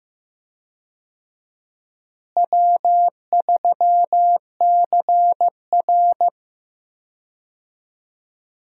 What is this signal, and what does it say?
Morse code, heard as a keyed tone, about 15 wpm, 700 Hz. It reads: W3CR
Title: USA Call Signs - 4 Characters - Repeated 2x - 15wpm